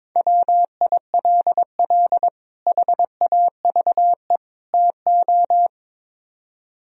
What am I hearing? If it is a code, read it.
Morse code, heard as a keyed tone, about 24 wpm, 700 Hz. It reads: WILL HAVE TO
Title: Speed-Racing - Sets of 3 Words 22wpm